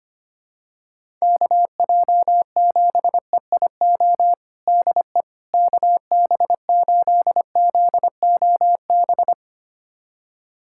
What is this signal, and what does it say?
Morse code, heard as a keyed tone, about 25 wpm, 700 Hz. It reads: KJ7EIO DE KB8ZOB
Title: USA Callsign DE Callsign 25wpm